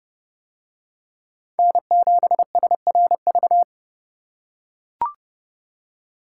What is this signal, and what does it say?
Morse code, heard as a keyed tone, about 29 wpm, 700 Hz. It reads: N7SRV